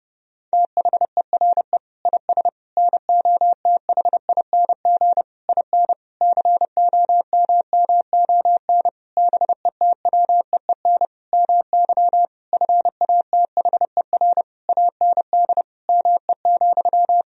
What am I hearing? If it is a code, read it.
Morse code, heard as a keyed tone, about 30 wpm, 700 Hz. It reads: THERE IS NOTHING IN COMMON BETWEEN MY FATHER AND ME,
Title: Sentences from Top 1000 Words - Review Part 3 - 30wpm